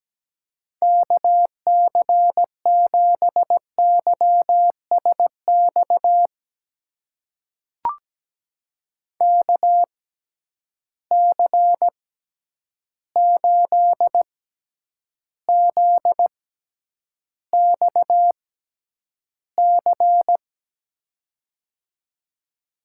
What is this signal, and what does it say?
Morse code, heard as a keyed tone, about 17 wpm, spaced as for 9 wpm, 700 Hz. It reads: KC7YSX KC8ZXC
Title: USA Call Signs - 6 Characters - 6x Character Spacing - 17wpm